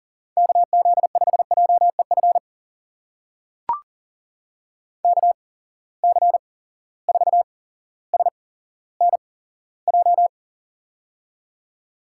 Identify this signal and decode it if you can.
Morse code, heard as a keyed tone, about 40 wpm, spaced as for 17 wpm, 700 Hz. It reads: KZ5JEF KC4SNJ
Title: USA Call Signs - 6 Characters - 8x Character Spacing - 40wpm